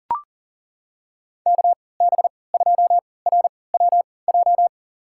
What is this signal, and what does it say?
Morse code, heard as a keyed tone, about 40 wpm, spaced as for 23 wpm, 700 Hz. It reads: KB2RWJ
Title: USA Call Signs - 6 Characters - 3x Character Spacing - 40wpm